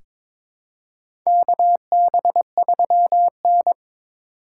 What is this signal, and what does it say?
Morse code, heard as a keyed tone, about 22 wpm, 700 Hz. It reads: KB3N
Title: USA Call Signs - 4 Characters - 4x Character Spacing - 22wpm